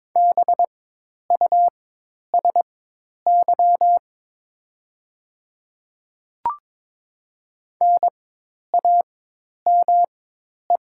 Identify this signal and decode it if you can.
Morse code, heard as a keyed tone, about 22 wpm, spaced as for 10 wpm, 700 Hz. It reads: BUSY NAME
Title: 4x Character Spacing - Top 500 Words - 22wpm